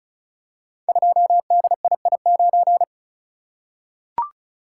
Morse code, heard as a keyed tone, about 36 wpm, 700 Hz. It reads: 2DII9
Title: Code Groups - 5 Letter:Numbers - 35wpm